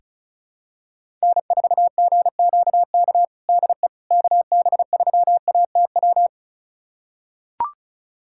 Morse code, heard as a keyed tone, about 34 wpm, 700 Hz. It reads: N4GQK DE KB3ATW
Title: USA Callsign DE Callsign 35wpm